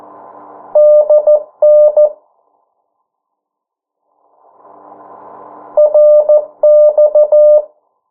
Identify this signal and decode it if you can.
Morse code, heard as a keyed tone, about 13 wpm, 600 Hz. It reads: DN RX
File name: morse mp3 / 09 urok 8